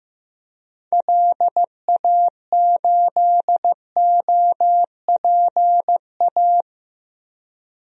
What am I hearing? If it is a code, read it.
Morse code, heard as a keyed tone, about 15 wpm, 700 Hz. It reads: LA8OPA
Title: DXCC-100 Call Signs - 6 Characters - Repeated 2x - 15wpm